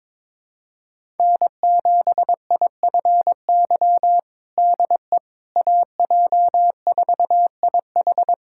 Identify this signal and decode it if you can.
Morse code, heard as a keyed tone, about 22 wpm, 700 Hz. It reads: N7IFY DE AJ4IH
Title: USA Callsign DE Callsign - 5 Characters - 22wpm